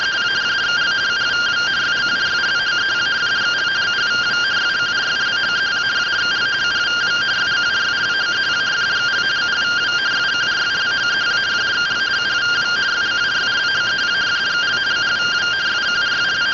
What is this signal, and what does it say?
Two-tone teleprinter signal, about 170 Hz shift.